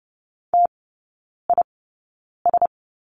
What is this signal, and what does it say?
Morse code, heard as a keyed tone, about 30 wpm, spaced as for 9 wpm, 700 Hz. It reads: TIS
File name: Lesson 17e - Letter u - Three Characters 30wpm